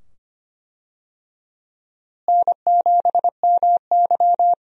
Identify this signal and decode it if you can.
Morse code, heard as a keyed tone, about 25 wpm, 700 Hz. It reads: N7MY
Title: USA Call Signs - 4 Characters - Repeated 2x - 25wpm